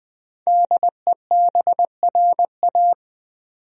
Morse code, heard as a keyed tone, about 20 wpm, 700 Hz. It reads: DEBRA
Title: USA Common Names - 20wpm